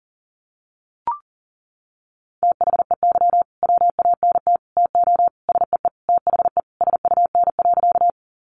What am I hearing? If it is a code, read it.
Morse code, heard as a keyed tone, about 37 wpm, 700 Hz. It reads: THEY WANT TO SEE THE SUN.
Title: Easy Phrases 40wpm